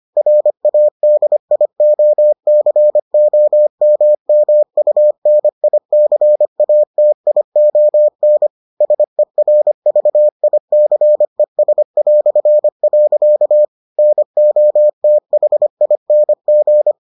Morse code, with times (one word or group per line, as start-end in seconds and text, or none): 0.2-8.5 RADIOCOMMUNICATION
8.8-13.6 SERVICES".
14.0-16.9 NOTHING